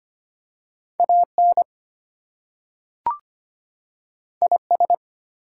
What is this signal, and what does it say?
Morse code, heard as a keyed tone, about 25 wpm, 700 Hz. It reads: AN IS
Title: Two Letter Words - 25wpm